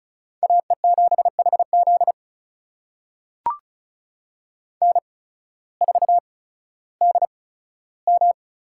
Morse code, heard as a keyed tone, about 34 wpm, spaced as for 16 wpm, 700 Hz. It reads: AE7HZ N4DM
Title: USA Call Signs - 5 Characters - 8x Character Spacing - 35wpm